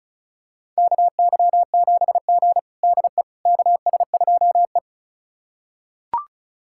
Morse code, heard as a keyed tone, about 36 wpm, 700 Hz. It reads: KY7G DE KS2E
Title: USA Callsign DE Callsign - 4 Characters - 35wpm